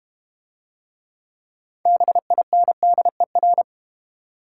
Morse code, heard as a keyed tone, about 31 wpm, 700 Hz. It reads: BINDER
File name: Easy Words - Speed-Racing with Hints - 25wpm